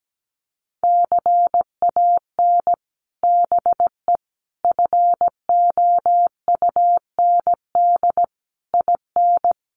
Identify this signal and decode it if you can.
Morse code, heard as a keyed tone, about 17 wpm, 700 Hz. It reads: CAN BE FOUND IN
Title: Sets of 5 Words Encore 17wpm